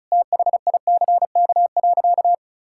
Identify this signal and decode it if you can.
Morse code, heard as a keyed tone, about 36 wpm, 700 Hz. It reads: THICK.
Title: Sentences from Top 1600 Words 35wpm